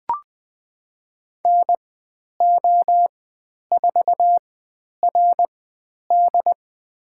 Morse code, heard as a keyed tone, about 20 wpm, spaced as for 10 wpm, 700 Hz. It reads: NO4RD
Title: 095 Lesson 16g - Letter c - Call Signs 20wpm